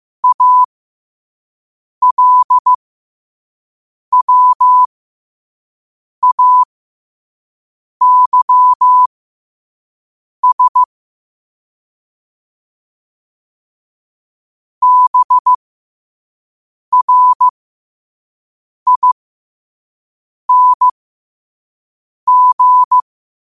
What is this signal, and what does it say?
Morse code, heard as a keyed tone, about 15 wpm, spaced as for 5 wpm, 1000 Hz. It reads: ALWAYS BRING
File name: qotdfiles / 5wpm